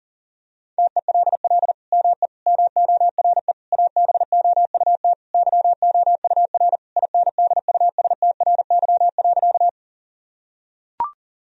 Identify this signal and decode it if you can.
Morse code, heard as a keyed tone, about 40 wpm, 700 Hz. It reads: TELL ME MORE ABOUT YOUR INDUSTRY.
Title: Sentences from Top 1000 Words - Review Part 4 - 40wpm